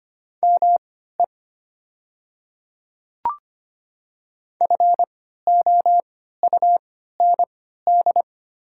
Morse code, heard as a keyed tone, about 25 wpm, spaced as for 14 wpm, 700 Hz. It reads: ME FOUND
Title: 3x Character Spacing - Top 300 Words - 25wpm